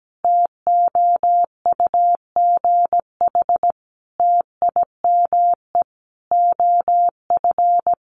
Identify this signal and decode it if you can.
Morse code, heard as a keyed tone, about 17 wpm, 700 Hz. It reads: TOUGH TIME OF
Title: Sentences from Top 1200 Words 17wpm